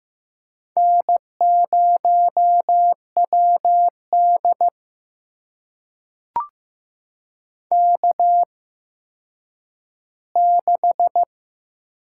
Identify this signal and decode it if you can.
Morse code, heard as a keyed tone, about 15 wpm, spaced as for 9 wpm, 700 Hz. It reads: N0WD K6
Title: USA Call Signs - 4 Characters - 8x Character Spacing - 15wpm